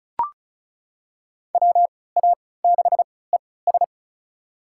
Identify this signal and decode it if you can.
Morse code, heard as a keyed tone, about 36 wpm, spaced as for 20 wpm, 700 Hz. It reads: WA6ES
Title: USA Call Signs - 5 Characters - 3x Character Spacing - 35wpm